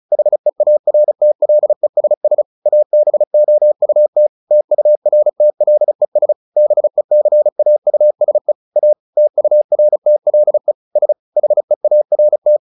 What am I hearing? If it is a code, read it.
Morse code, heard as a keyed tone, about 34 wpm, 600 Hz. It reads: HEARTLESS ABOUT TURTLES BECAUSE A TURTLE S HEART
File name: the old man and the sea / the old man and the sea 1 35wpm